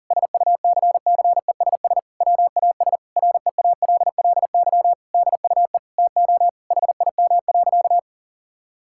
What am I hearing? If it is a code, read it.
Morse code, heard as a keyed tone, about 40 wpm, 700 Hz. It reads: SUCCESS WAS REALLY DUE TO HIM.
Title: Sentences from Top 1000 Words - Review Part 6 - 40wpm